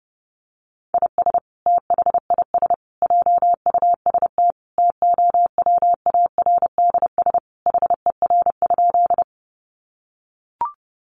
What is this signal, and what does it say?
Morse code, heard as a keyed tone, about 30 wpm, 700 Hz. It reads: IS THIS JUST TOWARDS HER?
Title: Sentences from Top 1100 Words 30wpm